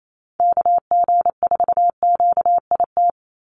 Morse code, heard as a keyed tone, about 28 wpm, 700 Hz. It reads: KG4QIT
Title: USA Call Signs 28wpm